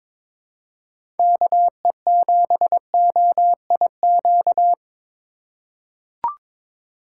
Morse code, heard as a keyed tone, about 21 wpm, 700 Hz. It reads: KE7OIQ
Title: USA Call Signs - 4x Character Spacing - 22wpm